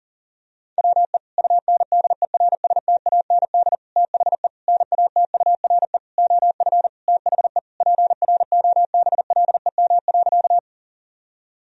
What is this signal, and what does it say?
Morse code, heard as a keyed tone, about 40 wpm, 700 Hz. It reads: WE UNDERSTAND THE NATURE OF THE PROBLEM.
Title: Sentences from Top 1000 Words - Review Part 2 - 40wpm